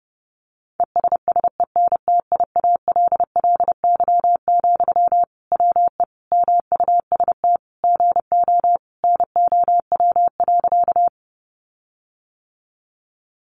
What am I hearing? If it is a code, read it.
Morse code, heard as a keyed tone, about 30 wpm, 700 Hz. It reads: ESSENTIALLY, WE MUST GO NOW.
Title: Sentences from Top 1300 Words 30wpm